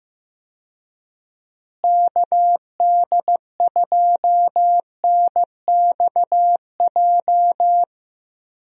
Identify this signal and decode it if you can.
Morse code, heard as a keyed tone, about 15 wpm, 700 Hz. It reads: KD2NXJ
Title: USA Call Signs - 6 Characters - Repeated 2x - 15wpm